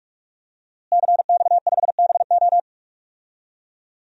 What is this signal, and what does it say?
Morse code, heard as a keyed tone, about 46 wpm, 700 Hz. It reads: CX5BO
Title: DXCC-100 Call Signs - 5 Characters - Repeated 3x - 45wpm